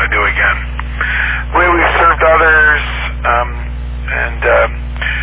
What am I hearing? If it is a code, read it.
Emergency Alert System (SAME) tones, heard as harsh buzzing bursts, with speech. It is heard from 0.9 to 4.4 s.